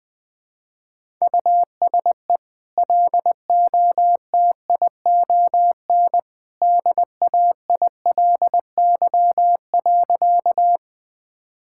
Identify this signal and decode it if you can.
Morse code, heard as a keyed tone, about 20 wpm, 700 Hz. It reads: USE LOTION DAILY.
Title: Easy Sentences with Hints - 20wpm